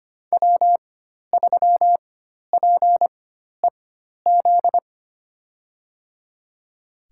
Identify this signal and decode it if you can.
Morse code, heard as a keyed tone, about 25 wpm, spaced as for 12 wpm, 700 Hz. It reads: W3PEZ
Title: USA Call Signs - 5 Characters - 4x Character Spacing - 25wpm